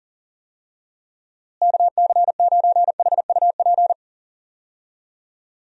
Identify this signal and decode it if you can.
Morse code, heard as a keyed tone, about 40 wpm, 700 Hz. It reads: KC9HUP
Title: USA Call Signs - 6 Characters - Repeated 3x - 40wpm